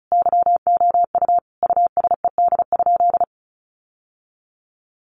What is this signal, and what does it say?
Morse code, heard as a keyed tone, about 34 wpm, 700 Hz. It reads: YOU USED?
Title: Sentences from Top 1200 Words 35wpm